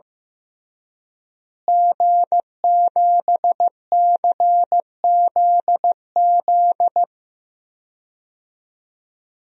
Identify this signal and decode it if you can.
Morse code, heard as a keyed tone, about 15 wpm, 700 Hz. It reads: G7CZZ
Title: DXCC-100 Call Signs - 5 Characters - Repeated 2x - 15wpm